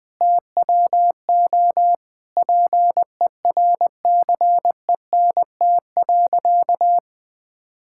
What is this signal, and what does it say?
Morse code, heard as a keyed tone, about 21 wpm, 700 Hz. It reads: TWO PERCENT.
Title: Sentences with 8-Word Phrases - 20wpm